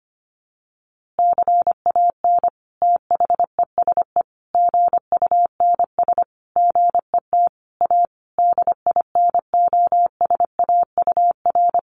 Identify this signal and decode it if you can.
Morse code, heard as a keyed tone, about 25 wpm, 700 Hz. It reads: CAN THESE GUNS GET A DINOSAUR
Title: A Sound Of Thunder 25wpm